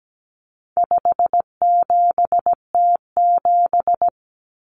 Morse code, heard as a keyed tone, about 17 wpm, 700 Hz. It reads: 57T7
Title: Common Cut Numbers - 4-Digit 17wpm